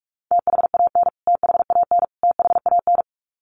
Morse code, heard as a keyed tone, about 44 wpm, 700 Hz. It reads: THAN THAN THAN
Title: Top 500 Words Repeated 45wpm